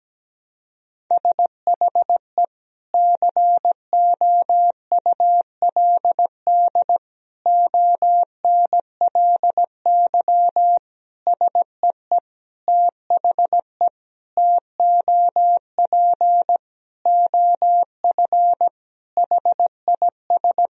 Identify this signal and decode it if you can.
Morse code, heard as a keyed tone, about 17 wpm, 700 Hz. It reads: SHE COULD ONLY SEE THE TOP OF HIS
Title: Sentences from Top 1000 Words - Review Part 2 - 17wpm